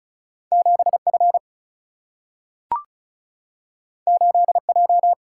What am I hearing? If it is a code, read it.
Morse code, heard as a keyed tone, about 36 wpm, 700 Hz. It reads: 7 F 8 J
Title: Single-Letter Single-Number 35wpm